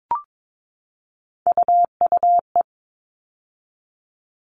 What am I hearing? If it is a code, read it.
Morse code, heard as a keyed tone, about 22 wpm, 700 Hz. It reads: UUE
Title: All Cut Numbers - 3-Digit 22wpm